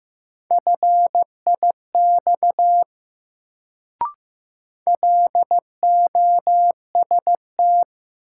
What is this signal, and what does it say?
Morse code, heard as a keyed tone, about 15 wpm, 700 Hz. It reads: FIX LOST
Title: Sentences from Top 1400 Words - New Word Drill - 15wpm